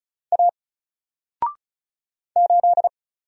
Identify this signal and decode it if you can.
Morse code, heard as a keyed tone, about 36 wpm, 700 Hz. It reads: A 8
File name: Single-Letter-Number-35wpm